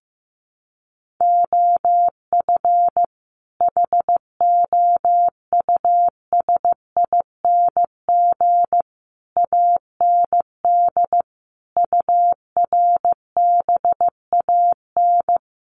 Morse code, heard as a keyed tone, about 14 wpm, 700 Hz. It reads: OF HOUSING AND URBAN
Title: Sets of 5 Words Encore2 15wpm